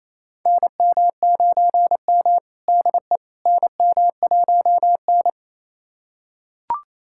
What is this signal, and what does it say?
Morse code, heard as a keyed tone, about 28 wpm, 700 Hz. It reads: NM9M DE NM1N
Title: USA Callsign DE Callsign - 4 Characters - 28wpm